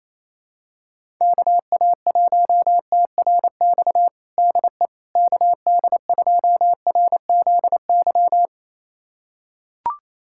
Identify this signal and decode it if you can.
Morse code, heard as a keyed tone, about 28 wpm, 700 Hz. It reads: KA1TRX DE KD2RZY
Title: USA Callsign DE Callsign - 6 Characters - 28wpm